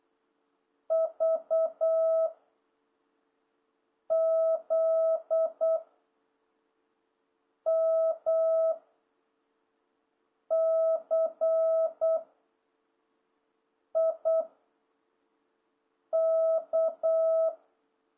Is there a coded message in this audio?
Yes: Morse code from 0.9 to 17.5 s, heard as a keyed tone, about 8 wpm, 660 Hz. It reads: V Z M C I K